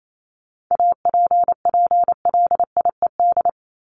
Morse code, heard as a keyed tone, about 28 wpm, 700 Hz. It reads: APPLIED
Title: ED Suffix - Top 100 Words with Base Words 28wpm